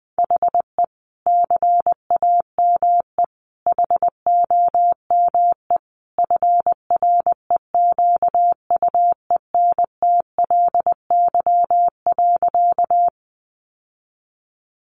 Morse code, heard as a keyed tone, about 20 wpm, 700 Hz. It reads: HE CAME HOME FREQUENTLY.
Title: Sentences from Top 1300 Words 20wpm